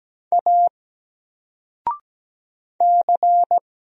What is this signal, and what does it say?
Morse code, heard as a keyed tone, about 17 wpm, 700 Hz. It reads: A C
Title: Single Letter-Number Alt 17wpm